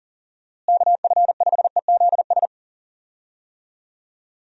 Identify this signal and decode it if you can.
Morse code, heard as a keyed tone, about 40 wpm, 700 Hz. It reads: KF5EZS
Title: USA Call Signs - 6 Characters - Repeated 3x - 40wpm